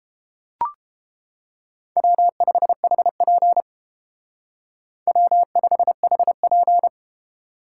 Morse code, heard as a keyed tone, about 31 wpm, 700 Hz. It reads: W5HP W5HP